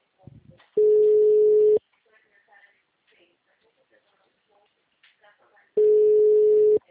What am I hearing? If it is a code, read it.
Ringback tone (Europe, 425 Hz), with speech.
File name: out- -2013- - - .22684